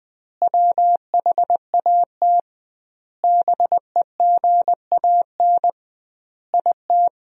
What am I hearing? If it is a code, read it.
Morse code, heard as a keyed tone, about 21 wpm, 700 Hz. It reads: WHAT BEGAN IT
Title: 2x Word-Spacing - Sentences from Top 300 Words - 20wpm